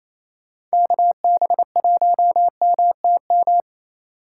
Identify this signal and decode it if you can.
Morse code, heard as a keyed tone, about 28 wpm, 700 Hz. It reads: KB1MTM
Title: USA Call Signs - 3x Character Spacing - 28wpm